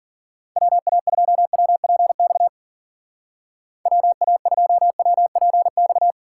Morse code, heard as a keyed tone, about 50 wpm, 700 Hz. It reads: WA2WPX WA2WPX
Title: Speed Racing - USA Call Signs 40wpm